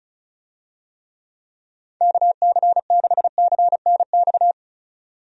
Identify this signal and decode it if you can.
Morse code, heard as a keyed tone, about 34 wpm, 700 Hz. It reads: KC6CNX
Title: USA Call Signs - 6 Characters - 35wpm